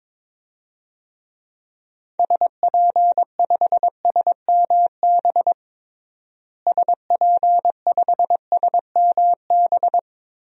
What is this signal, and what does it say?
Morse code, heard as a keyed tone, about 22 wpm, 700 Hz. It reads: SP5SMB SP5SMB
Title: DXCC-100 Call Signs - 6 Characters - Repeated 2x - 22wpm